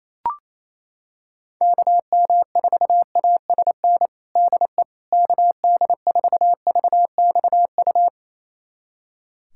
Morse code, heard as a keyed tone, about 28 wpm, 700 Hz. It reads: KM4ASN DE KD4VXU
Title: USA Callsign DE Callsign - 6 Characters - 28wpm